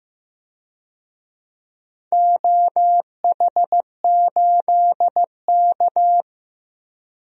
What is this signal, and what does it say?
Morse code, heard as a keyed tone, about 15 wpm, 700 Hz. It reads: OH8K
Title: DXCC-100 Call Signs - 4 Characters - Repeated 3x - 15wpm